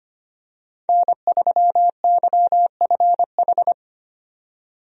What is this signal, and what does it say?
Morse code, heard as a keyed tone, about 25 wpm, 700 Hz. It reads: N3YFH
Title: USA Call Signs - 5 Characters - 25wpm